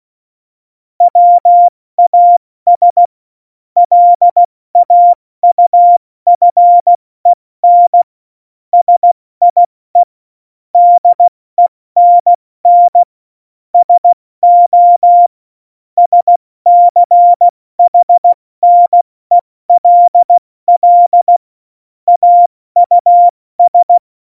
Morse code, heard as a keyed tone, about 16 wpm, 700 Hz. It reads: WAS LAUFEN SIE DENN SO SCHNELL AUS
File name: Morsekurs / Witze / Witz 15. September 2020, 70 F80